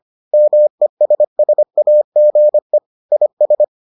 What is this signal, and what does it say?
Morse code, heard as a keyed tone, about 25 wpm, 600 Hz. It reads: MESSAGE IS